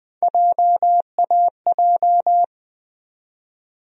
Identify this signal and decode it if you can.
Morse code, heard as a keyed tone, about 20 wpm, 700 Hz. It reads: JAJ
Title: USA Call Sign Postfixes - 3 Letters 20wpm